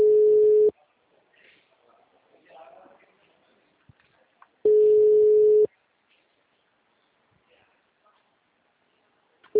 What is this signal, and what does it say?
Ringback tone (Europe, 425 Hz), with speech.